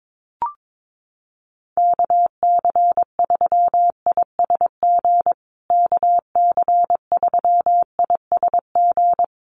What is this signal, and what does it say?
Morse code, heard as a keyed tone, about 22 wpm, 700 Hz. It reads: KC3ISG KC3ISG
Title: USA Call Signs Repeated 3x 22wpm